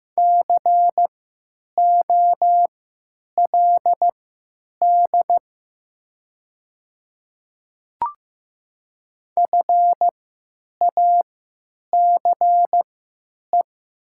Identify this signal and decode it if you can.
Morse code, heard as a keyed tone, about 15 wpm, spaced as for 9 wpm, 700 Hz. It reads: COLD FACE
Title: 3x Character Spacing - Top 500 Words - 15wpm